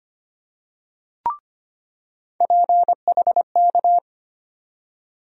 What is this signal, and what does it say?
Morse code, heard as a keyed tone, about 25 wpm, 700 Hz. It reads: PHK
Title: USA Call Sign Postfixes - 3 Letters 25wpm